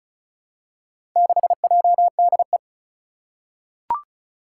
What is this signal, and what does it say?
Morse code, heard as a keyed tone, about 36 wpm, 690 Hz. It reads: 6JDE
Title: Four Challenging Letter-Number Combinations 35wpm